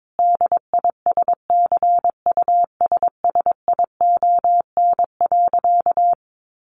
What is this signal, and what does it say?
Morse code, heard as a keyed tone, about 22 wpm, 700 Hz. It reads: DISCUSSION.
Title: Sentences from Top 1100 Words 22wpm